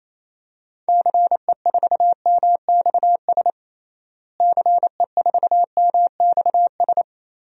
Morse code, heard as a keyed tone, about 28 wpm, 700 Hz. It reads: CE4MXS CE4MXS
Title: DXCC-100 Call Signs - 6 Characters - Repeated 3x - 28wpm